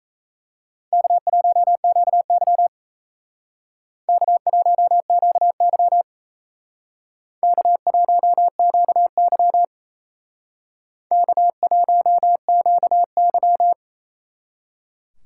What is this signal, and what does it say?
Morse code, heard as a keyed tone, about 35 wpm, 700 Hz. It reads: K1QY K1QY K1QY K1QY